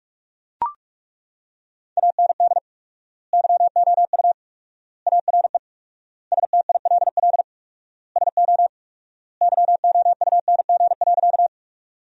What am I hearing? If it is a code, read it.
Morse code, heard as a keyed tone, about 41 wpm, 700 Hz. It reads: AND YOU ARE STILL SO YOUNG.
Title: 4x Word-Spacing - Sentences from Top 400 Words - 45wpm